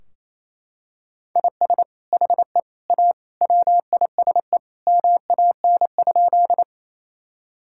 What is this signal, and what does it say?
Morse code, heard as a keyed tone, about 28 wpm, 700 Hz. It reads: IS HE A WISE MAN?
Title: Sentences from Top 1600 Words 28wpm